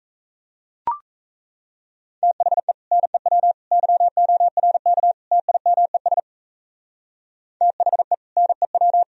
Morse code, heard as a keyed tone, about 32 wpm, 700 Hz. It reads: THE NEW YORK TIMES THE NEW
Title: Speed-Racing - Sets of 4 Words 28wpm